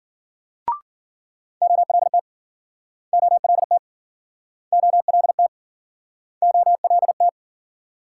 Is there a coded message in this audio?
Yes: Morse code from 1.6 to 7.3 s, heard as a keyed tone, about 53 wpm, 700 Hz. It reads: OLT OLT OLT OLT